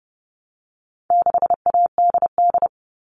Morse code, heard as a keyed tone, about 30 wpm, 700 Hz. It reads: 6ADD
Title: All Cut Numbers - 4-Digit 30wpm